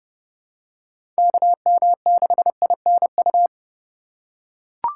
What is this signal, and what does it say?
Morse code, heard as a keyed tone, about 32 wpm, 700 Hz. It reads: KM6INU
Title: USA Call Signs - 6 Characters - 4x Character Spacing - 30wpm